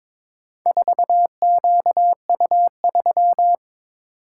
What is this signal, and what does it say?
Morse code, heard as a keyed tone, about 22 wpm, 700 Hz. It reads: 4QU3
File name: practice / 22wpm / Four Letter-Number Combinations - 22wpm